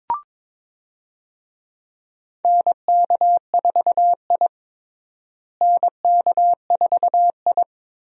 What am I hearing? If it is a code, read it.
Morse code, heard as a keyed tone, about 21 wpm, 700 Hz. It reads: NK4I NK4I
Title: USA Call Signs - 4 Characters - Repeated 2x - 22wpm